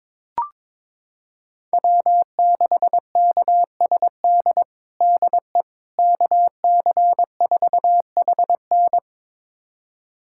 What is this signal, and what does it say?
Morse code, heard as a keyed tone, about 22 wpm, 700 Hz. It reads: W6KSD DE KC4HN
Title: USA Callsign DE Callsign - 5 Characters - 22wpm